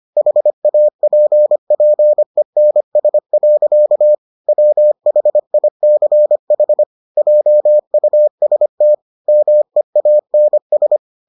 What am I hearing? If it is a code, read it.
Morse code, heard as a keyed tone, about 25 wpm, 600 Hz. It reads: HAPPENS. WHICH JUST MEANS